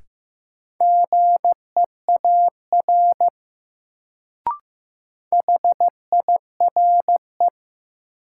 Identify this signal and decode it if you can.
Morse code, heard as a keyed tone, about 15 wpm, 700 Hz. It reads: GEAR HIRE
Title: Sentences from Top 1400 Words - New Word Drill - 15wpm